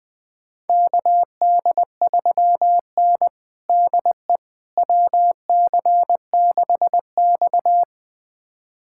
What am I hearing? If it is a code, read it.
Morse code, heard as a keyed tone, about 20 wpm, 700 Hz. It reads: KD3N DE WC6X
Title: USA Callsign DE Callsign - 4 Characters - 20wpm